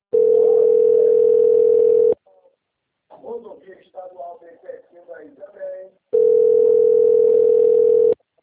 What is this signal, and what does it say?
Ringback tone (North America), 440 + 480 Hz, with speech.
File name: out- - - - - .51956